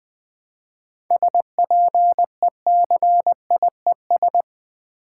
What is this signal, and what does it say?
Morse code, heard as a keyed tone, about 20 wpm, 700 Hz. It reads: SPECIES
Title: 7 Letter Words - First 3 Letters Given - 20wpm